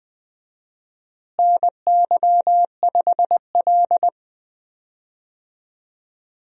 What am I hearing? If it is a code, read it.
Morse code, heard as a keyed tone, about 21 wpm, 700 Hz. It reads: NY5L
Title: USA Call Signs - 4 Characters - Repeated 2x - 20wpm